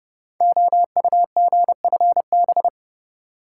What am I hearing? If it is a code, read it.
Morse code, heard as a keyed tone, about 30 wpm, 700 Hz. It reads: OUGFB
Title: Code Groups - 5 Letters - 30wpm